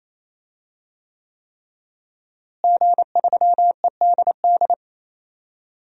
Morse code, heard as a keyed tone, about 28 wpm, 700 Hz. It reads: G3EDD
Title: DXCC-100 Call Signs - 5 Characters - Repeated 3x - 28wpm